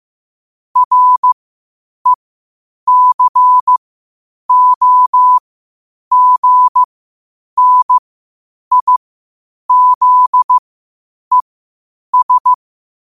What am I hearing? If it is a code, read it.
Morse code, heard as a keyed tone, about 15 wpm, spaced as for 8 wpm, 1000 Hz. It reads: RECOGNIZES
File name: qotdfiles / 8wpm